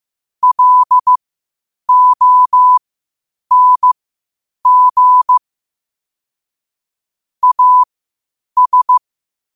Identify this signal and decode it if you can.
Morse code, heard as a keyed tone, about 15 wpm, spaced as for 8 wpm, 1000 Hz. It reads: LONG AS
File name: qotdfiles / 8wpm